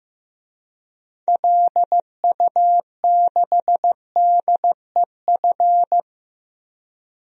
Morse code, heard as a keyed tone, about 15 wpm, 700 Hz. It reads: LU6DEF
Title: DXCC-100 Call Signs - 6 Characters - Repeated 2x - 15wpm